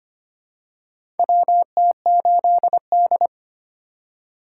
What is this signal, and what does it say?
Morse code, heard as a keyed tone, about 25 wpm, 700 Hz. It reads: WT8D
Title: USA Call Signs - 4x Character Spacing - 25wpm